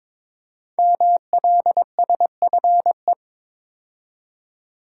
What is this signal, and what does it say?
Morse code, heard as a keyed tone, about 22 wpm, 700 Hz. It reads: MLSFE
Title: Code Groups - 5 Letters - 22wpm